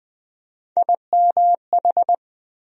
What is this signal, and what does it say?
Morse code, heard as a keyed tone, about 20 wpm, 700 Hz. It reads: IMH
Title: USA Call Sign Postfixes - 3 Letters 20wpm